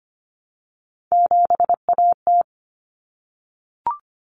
Morse code, heard as a keyed tone, about 25 wpm, 700 Hz. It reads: 7AT
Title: Common Cut Numbers - 3-Digit 25wpm